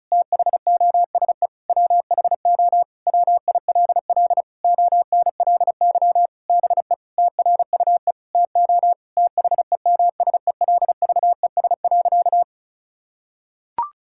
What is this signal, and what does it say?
Morse code, heard as a keyed tone, about 36 wpm, 700 Hz. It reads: THOSE WHO WILL ONLY BE TRUE TO THEMSELVES.
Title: Sentences from Top 1000 Words - Review Part 6 - 35wpm